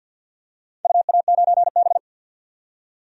Morse code, heard as a keyed tone, about 53 wpm, 700 Hz. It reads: UA9B
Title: DXCC-100 Call Signs - 4 Characters - 50wpm